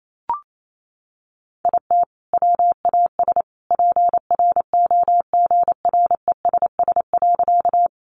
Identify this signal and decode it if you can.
Morse code, heard as a keyed tone, about 28 wpm, 700 Hz. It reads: IT WAS PROGRESS.
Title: Sentences from Top 1400 Words 28wpm